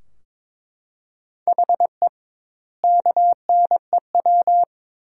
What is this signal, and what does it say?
Morse code, heard as a keyed tone, about 21 wpm, 700 Hz. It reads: HE KNEW
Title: 2x Word-Spacing - Sentences from Top 400 Words - 22wpm